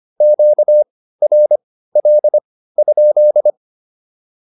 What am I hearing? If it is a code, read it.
Morse code, heard as a keyed tone, about 25 wpm, spaced as for 15 wpm, 600 Hz. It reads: QRL?